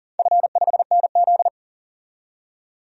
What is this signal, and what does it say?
Morse code, heard as a keyed tone, about 40 wpm, 700 Hz. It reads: F5NZ
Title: DXCC-100 Call Signs - 4 Characters - 40wpm